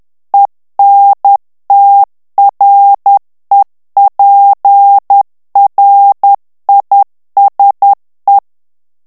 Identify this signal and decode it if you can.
Morse code, heard as a keyed tone, about 10 wpm, 800 Hz. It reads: ENTREPRISE
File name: CW / FICHIERS-MP3 / FRANCAIS TEXTE CLAIR 2 VITESSE 10 MOTS MINUTES